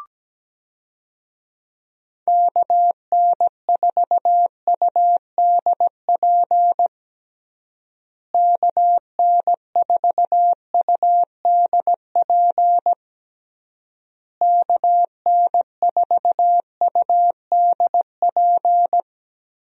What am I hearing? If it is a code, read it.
Morse code, heard as a keyed tone, about 17 wpm, 700 Hz. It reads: KN4UDP KN4UDP KN4UDP
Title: USA Call Signs - 6 Characters - Repeated 3x - 17wpm